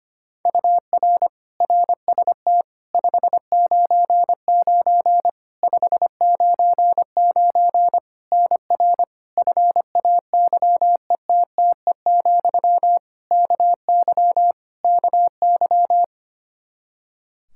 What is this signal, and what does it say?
Morse code, heard as a keyed tone, about 25 wpm, 700 Hz. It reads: UR RST 599 599 NR FAYETTE, KY KY
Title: Practice QSO - Exchange 1 - Beginning - 25wpm